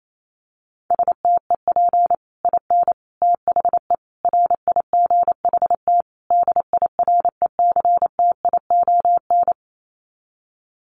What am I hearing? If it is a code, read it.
Morse code, heard as a keyed tone, about 28 wpm, 700 Hz. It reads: STEP IN THE RIGHT DIRECTION
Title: Sets of 5 Words Encore 28wpm